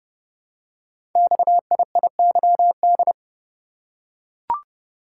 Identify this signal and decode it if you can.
Morse code, heard as a keyed tone, about 28 wpm, 700 Hz. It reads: XIIYD
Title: Code Groups - 5 Letter:Numbers - 30wpm